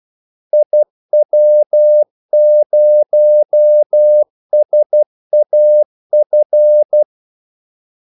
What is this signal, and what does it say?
Morse code, heard as a keyed tone, about 12 wpm, 600 Hz. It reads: IW0SAF
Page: 2 ekstra ordmellemrum mellem kaldesignalerne